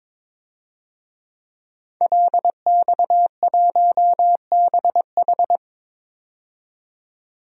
Morse code, heard as a keyed tone, about 21 wpm, 700 Hz. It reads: LX1BH
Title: DXCC-100 Call Signs - 5 Characters - 22wpm